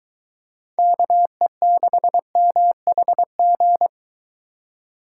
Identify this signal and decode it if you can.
Morse code, heard as a keyed tone, about 22 wpm, 700 Hz. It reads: KE6MHG